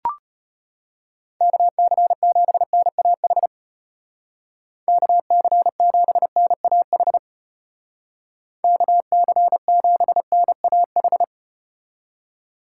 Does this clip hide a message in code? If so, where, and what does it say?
Yes: Morse code from 1.4 to 11.2 s, heard as a keyed tone, about 35 wpm, 700 Hz. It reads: KC7NAH KC7NAH KC7NAH